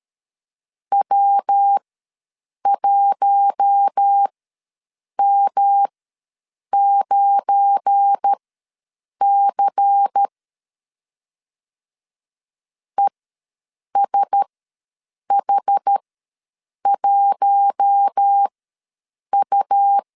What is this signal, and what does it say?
Morse code, heard as a keyed tone, about 13 wpm, spaced as for 7 wpm, 800 Hz. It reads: W1M9C ESH1U
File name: LEZIONI / 7 GF9Z / 7 GF9 ws 12wpm 3344